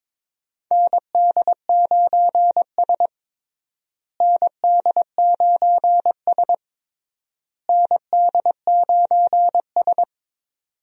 Morse code, heard as a keyed tone, about 22 wpm, 700 Hz. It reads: ND9S ND9S ND9S
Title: USA Call Signs - 4 Characters - Repeated 3x - 22wpm